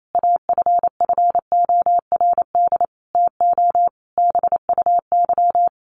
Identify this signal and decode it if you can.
Morse code, heard as a keyed tone, about 28 wpm, 700 Hz. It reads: AFFORD TO BUY
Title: Sentences from Top 1100 Words 28wpm